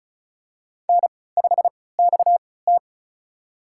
Morse code, heard as a keyed tone, about 34 wpm, spaced as for 20 wpm, 700 Hz. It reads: N5XT
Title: USA Call Signs - 4 Characters - 3x Character Spacing - 35wpm